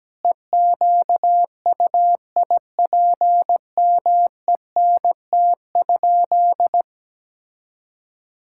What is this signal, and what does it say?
Morse code, heard as a keyed tone, about 17 wpm, 700 Hz. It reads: EQUIPMENT?
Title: Sentences from Top 1000 Words - Review Part 5 - 17wpm